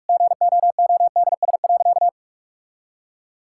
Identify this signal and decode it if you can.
Morse code, heard as a keyed tone, about 46 wpm, 700 Hz. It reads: GOODS.
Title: Sentences from Top 1000 Words - Review Part 6 - 45wpm